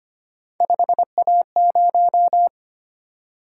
Morse code, heard as a keyed tone, about 29 wpm, 700 Hz. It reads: 5A0
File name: Speed-Racing - Three Letter-Number Combinations - 25wpm